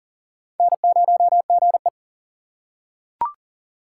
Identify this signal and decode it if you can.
Morse code, heard as a keyed tone, about 39 wpm, 700 Hz. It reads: N0GE
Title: USA Call Signs - 3x Character Spacing - 40wpm